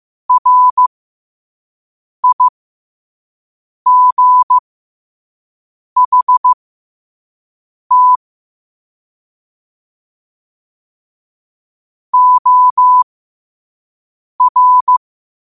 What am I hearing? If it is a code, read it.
Morse code, heard as a keyed tone, about 15 wpm, spaced as for 5 wpm, 1000 Hz. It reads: RIGHT OR
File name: qotdfiles / 5wpm